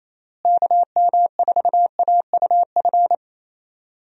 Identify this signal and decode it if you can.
Morse code, heard as a keyed tone, about 27 wpm, 700 Hz. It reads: KM4AUF
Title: USA Call Signs - 6 Characters - Repeated 3x - 28wpm